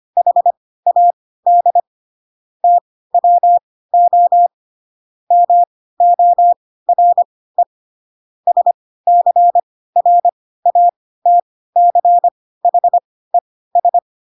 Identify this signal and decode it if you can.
Morse code, heard as a keyed tone, about 25 wpm, spaced as for 16 wpm, 700 Hz. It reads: HAD TWO MORE SCRATCHES
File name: Pinocchio / Pinocchio ch2 10 25 ik1hge